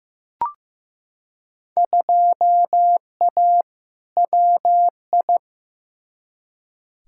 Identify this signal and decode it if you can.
Morse code, heard as a keyed tone, about 16 wpm, 700 Hz. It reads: 2A WI
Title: Field Day Exchanges - 15wpm